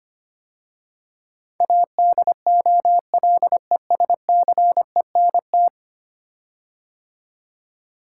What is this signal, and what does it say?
Morse code, heard as a keyed tone, about 25 wpm, 700 Hz. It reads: ADOLESCENT
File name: 10 Letter Words - First 3 Letters Given - 25wpm